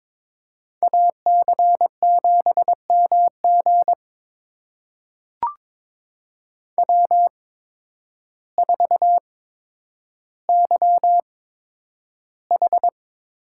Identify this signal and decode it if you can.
Morse code, heard as a keyed tone, about 22 wpm, spaced as for 10 wpm, 700 Hz. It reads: AC7MG W4YH
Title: USA Call Signs - 8x Character Spacing - 22wpm